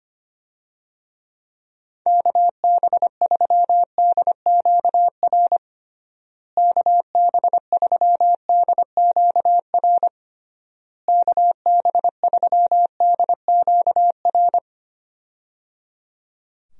Morse code, heard as a keyed tone, about 25 wpm, 700 Hz. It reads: KB3DQR KB3DQR KB3DQR
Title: USA Call Signs - 6 Characters - Repeated 3x - 25wpm